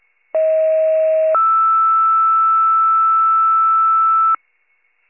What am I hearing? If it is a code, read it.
Two-tone page, heard as a one-second tone, then a three-second tone, about 640 then 1320 Hz. It is heard from 0.3 to 4.4 s.